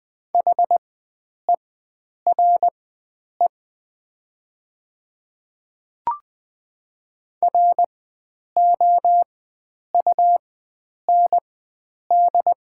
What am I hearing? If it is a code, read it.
Morse code, heard as a keyed tone, about 20 wpm, spaced as for 9 wpm, 700 Hz. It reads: HERE ROUND
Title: 4x Character Spacing - Top 300 Words - 20wpm